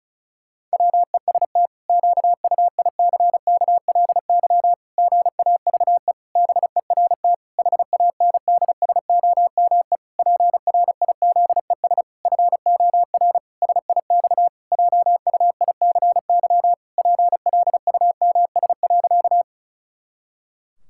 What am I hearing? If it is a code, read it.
Morse code, heard as a keyed tone, about 34 wpm, 700 Hz. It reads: WEST QUICKLY GAVE BERT HANDSOME PRIZES FOR SIX JUICY PLUMS.
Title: Pangram Challenge v2 - 35wpm